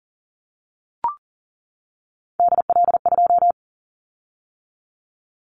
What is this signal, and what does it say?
Morse code, heard as a keyed tone, about 40 wpm, 700 Hz. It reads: DL2
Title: Callsign Prefixes for Top 50 DXCC - 40wpm